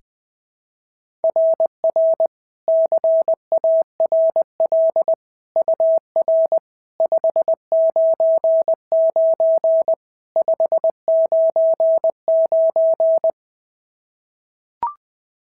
Morse code, heard as a keyed tone, about 19 wpm, 650 Hz. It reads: RR CARL UR 599 599
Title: SOTA QSO Practice 20wpm